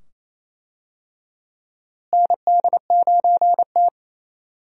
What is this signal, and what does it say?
Morse code, heard as a keyed tone, about 27 wpm, 700 Hz. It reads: ND9T
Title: USA Call Signs - 4 Characters - Repeated 2x - 28wpm